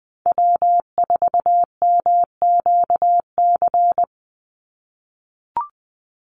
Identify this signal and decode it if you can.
Morse code, heard as a keyed tone, about 20 wpm, 700 Hz. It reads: W4MQC
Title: Call Signs - 20wpm